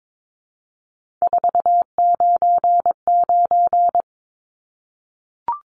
Morse code, heard as a keyed tone, about 22 wpm, 700 Hz. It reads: 499
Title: Numbers - 3-Digit 22wpm